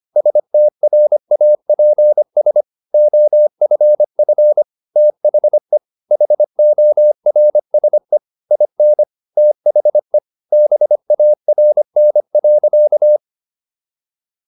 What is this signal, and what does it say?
Morse code, heard as a keyed tone, about 25 wpm, 600 Hz. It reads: STRAPS OFF THE HORSE IN THE BARN.